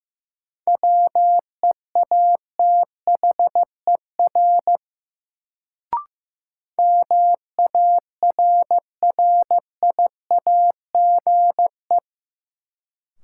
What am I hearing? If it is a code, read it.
Morse code, heard as a keyed tone, about 15 wpm, 700 Hz. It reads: WEATHER MARRIAGE
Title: Sentences from Top 800 Words - New Word Drill - 15wpm